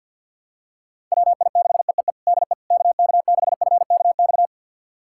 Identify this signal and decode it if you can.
Morse code, heard as a keyed tone, about 47 wpm, 700 Hz. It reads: WI6EEE DE KK6FKX
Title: USA Callsign DE Callsign - 6 Characters - 50wpm